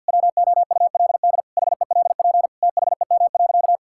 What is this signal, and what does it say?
Morse code, heard as a keyed tone, about 50 wpm, 700 Hz. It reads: WOULD HELP THEM.
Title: Sentences from Top 1000 Words - Review Part 5 - 50wpm